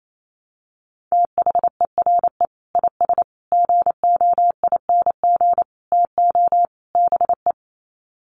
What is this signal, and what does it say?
Morse code, heard as a keyed tone, about 28 wpm, 700 Hz. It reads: THERE IS GOING TO BE
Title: Sets of 5 Words Encore 28wpm